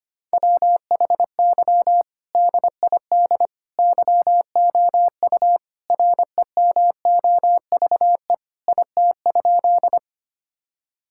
Morse code, heard as a keyed tone, about 25 wpm, 700 Hz. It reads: WHY DID YOU REMOVE IT?
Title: Sentences from Top 1000 Words - Review Part 5 - 25wpm